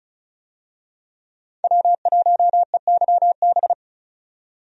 Morse code, heard as a keyed tone, about 36 wpm, 700 Hz. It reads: W1EYB
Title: USA Call Signs - 5 Characters - 35wpm